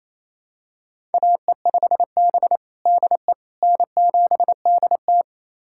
Morse code, heard as a keyed tone, about 28 wpm, 700 Hz. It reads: AE5B DE N7DT
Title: USA Callsign DE Callsign - 4 Characters - 28wpm